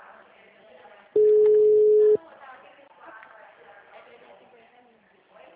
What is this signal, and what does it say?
Ringback tone (Europe, 425 Hz), with speech.